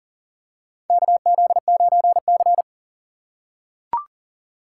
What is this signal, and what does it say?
Morse code, heard as a keyed tone, about 40 wpm, 700 Hz. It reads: KZ9C
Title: USA Call Signs - 4 Characters - 4x Character Spacing - 40wpm